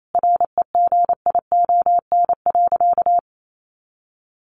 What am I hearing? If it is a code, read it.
Morse code, heard as a keyed tone, about 28 wpm, 700 Hz. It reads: REGION.
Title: Sentences from Top 1500 Words 28wpm